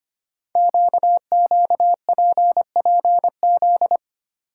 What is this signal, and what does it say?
Morse code, heard as a keyed tone, about 25 wpm, 700 Hz. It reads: QQPPZ
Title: Code Groups - 5 Letters - 25wpm